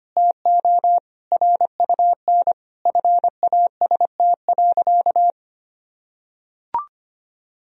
Morse code, heard as a keyed tone, about 25 wpm, 700 Hz. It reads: TO RUN FAST.
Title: Sentences from Top 1000 Words - Review Part 2 - 25wpm